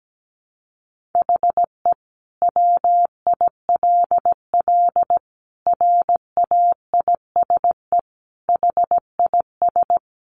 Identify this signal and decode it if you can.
Morse code, heard as a keyed tone, about 17 wpm, 700 Hz. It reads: HE WILL RAISE HIS
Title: Sentences from Top 1400 Words 17wpm